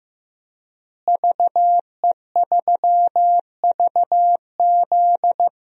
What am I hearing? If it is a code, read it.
Morse code, heard as a keyed tone, about 15 wpm, 700 Hz. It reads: VE3VZ
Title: DXCC-100 Call Signs - 5 Characters - Repeated 2x - 15wpm